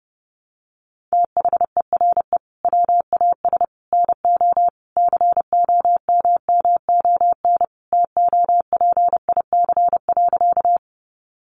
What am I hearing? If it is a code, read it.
Morse code, heard as a keyed tone, about 30 wpm, 700 Hz. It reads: THERE WAS NO COMMON TOPIC.
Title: Sentences from Top 1200 Words 30wpm